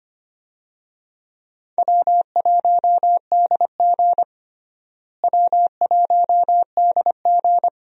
Morse code, heard as a keyed tone, about 25 wpm, 700 Hz. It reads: W1DG W1DG
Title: USA Call Signs - 4 Characters - Repeated 3x - 25wpm